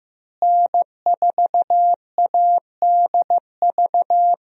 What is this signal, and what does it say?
Morse code, heard as a keyed tone, about 15 wpm, 700 Hz. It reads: N4ADV
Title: USA Call Signs - 5 Characters - Repeated 3x - 15wpm